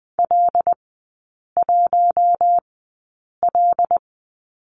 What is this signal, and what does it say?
Morse code, heard as a keyed tone, about 20 wpm, spaced as for 8 wpm, 700 Hz. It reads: L1L
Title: 074 Lesson 13e - Letter L - Three Characters 20wpm